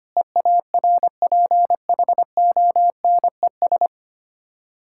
Morse code, heard as a keyed tone, about 25 wpm, 700 Hz. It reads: EARPHONES
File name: practice / 25wpm / Difficult Words with Hints - 25wpm